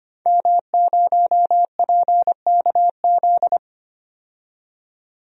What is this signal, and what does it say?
Morse code, heard as a keyed tone, about 25 wpm, 700 Hz. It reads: M0PKZ
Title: DXCC-100 Call Signs - 5 Characters - 25wpm